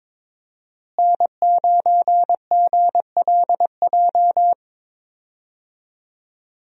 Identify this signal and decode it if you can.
Morse code, heard as a keyed tone, about 22 wpm, 700 Hz. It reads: N9GLJ
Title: USA Call Signs - 5 Characters - Repeated 3x - 22wpm